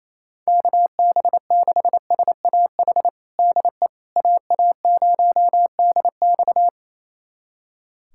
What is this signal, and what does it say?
Morse code, heard as a keyed tone, about 28 wpm, 700 Hz. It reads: KB6SAH DE AA0DX
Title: USA Callsign DE Callsign 28wpm